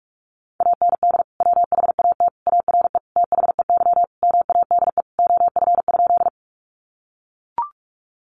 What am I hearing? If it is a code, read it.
Morse code, heard as a keyed tone, about 44 wpm, 700 Hz. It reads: AND WHAT ARE THEY MADE OF?
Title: Sentences from Top 200 Words 45wpm